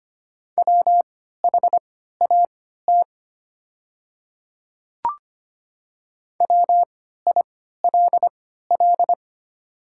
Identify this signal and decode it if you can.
Morse code, heard as a keyed tone, about 25 wpm, spaced as for 14 wpm, 700 Hz. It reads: WHAT WILL
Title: 3x Character Spacing - Top 100 Words - 25wpm